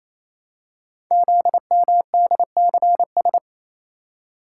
Morse code, heard as a keyed tone, about 27 wpm, 700 Hz. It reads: ZMDCS
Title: Code Groups - 5 Letters - 28wpm